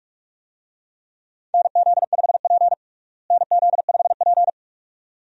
Morse code, heard as a keyed tone, about 46 wpm, 700 Hz. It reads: NZ5P NZ5P
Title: USA Call Signs - 4 Characters - Repeated 2x - 45wpm